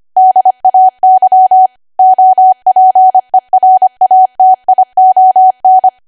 Morse code, heard as a keyed tone, about 25 wpm, 750 Hz. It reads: DAY OPERATION
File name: cwa / W1AW-Level3 / 25WPM